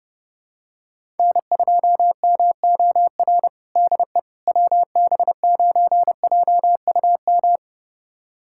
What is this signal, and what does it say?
Morse code, heard as a keyed tone, about 32 wpm, 700 Hz. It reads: N2MOR DE WB9JUM
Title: USA Callsign DE Callsign 30wpm